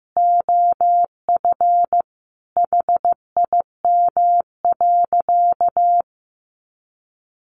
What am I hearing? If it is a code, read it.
Morse code, heard as a keyed tone, about 15 wpm, 700 Hz. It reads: OF HIM.
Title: Sentences from Top 1500 Words 15wpm